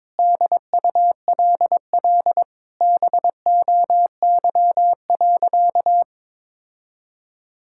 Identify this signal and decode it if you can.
Morse code, heard as a keyed tone, about 22 wpm, 700 Hz. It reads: DULL BOY.
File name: Sentences with 7-Word Phrases - 22wpm